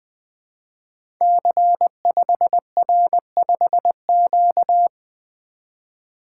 Morse code, heard as a keyed tone, about 20 wpm, 700 Hz. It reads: C5R5Q
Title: Code Groups - 5 Letter:Numbers - 20wpm